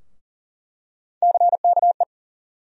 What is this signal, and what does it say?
Morse code, heard as a keyed tone, about 40 wpm, 700 Hz. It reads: CKE
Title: USA Call Sign Postfixes - 3 Letters 40wpm